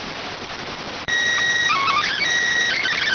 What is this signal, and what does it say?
SSTV (slow-scan television), PD 290.